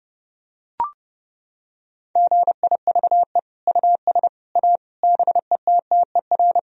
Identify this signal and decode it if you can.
Morse code, heard as a keyed tone, about 30 wpm, 700 Hz. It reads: GIVE US A BETTER
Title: Sentences from Top 1000 Words - Review Part 4 - 30wpm